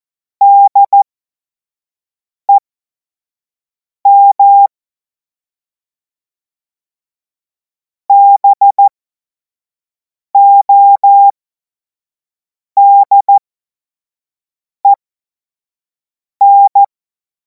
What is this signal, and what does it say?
Morse code, heard as a keyed tone, about 14 wpm, spaced as for 5 wpm, 800 Hz. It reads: DEM BODEN
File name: Morsekurs / Witze / Witz 17. Februar 2026 25 F70